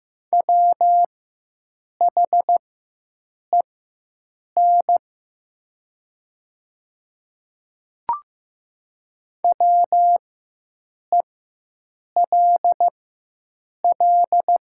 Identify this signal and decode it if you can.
Morse code, heard as a keyed tone, about 15 wpm, spaced as for 7 wpm, 700 Hz. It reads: WHEN WELL
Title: 4x Character Spacing - Top 200 Words - 15wpm